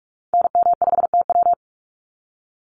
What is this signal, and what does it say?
Morse code, heard as a keyed tone, about 44 wpm, 700 Hz. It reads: NM5TW
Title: NA SOTA Activator Callsigns 45wpm